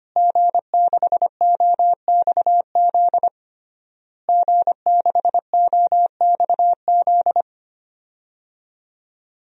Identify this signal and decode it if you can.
Morse code, heard as a keyed tone, about 25 wpm, 700 Hz. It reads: G6OXZ G6OXZ
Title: DXCC-100 Call Signs - 5 Characters - Repeated 3x - 25wpm